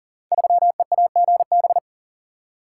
Morse code, heard as a keyed tone, about 40 wpm, 700 Hz. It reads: 3EAGB
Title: Code Groups - 5 Letter:Numbers - 40wpm